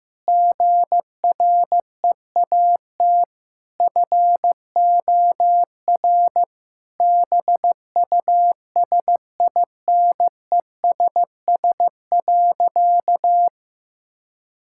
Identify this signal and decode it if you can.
Morse code, heard as a keyed tone, about 15 wpm, 700 Hz. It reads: GREAT FOR BUSINESS.
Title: Sentences from Top 1000 Words - Review Part 5 - 15wpm